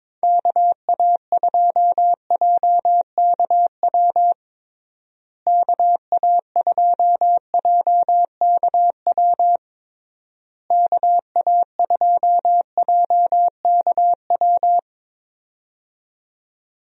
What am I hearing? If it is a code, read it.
Morse code, heard as a keyed tone, about 21 wpm, 700 Hz. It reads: KA2JKW KA2JKW KA2JKW
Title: USA Call Signs - 6 Characters - Repeated 3x - 22wpm